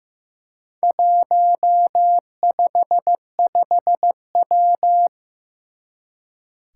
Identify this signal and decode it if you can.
Morse code, heard as a keyed tone, about 15 wpm, 700 Hz. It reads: 155W
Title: Four Challenging Letter-Number Combinations 15wpm